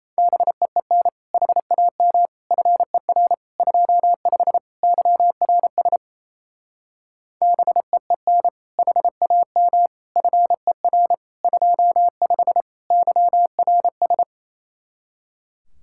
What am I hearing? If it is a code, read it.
Morse code, heard as a keyed tone, about 34 wpm, 700 Hz. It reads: BEEN HAM FER 25 YRS BEEN HAM FER 25 YRS
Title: Speed-Racing Common QSO Phrases - 28wpm